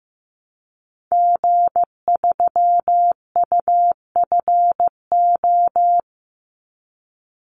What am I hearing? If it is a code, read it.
Morse code, heard as a keyed tone, about 15 wpm, 700 Hz. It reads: G3UFO
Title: Call Signs Repeated 3x - 15wpm